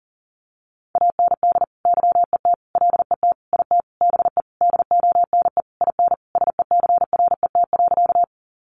Morse code, heard as a keyed tone, about 40 wpm, 700 Hz. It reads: AND YET LET IT BE DONE IN SECRET.
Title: Sentences from Top 1200 Words 40wpm